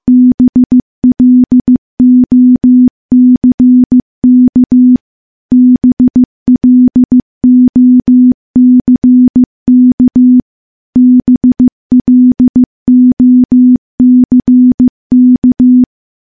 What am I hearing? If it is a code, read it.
Morse code, heard as a keyed tone, about 15 wpm, 250 Hz. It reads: BLOCK BLOCK BLOCK
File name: morse-training-15WPM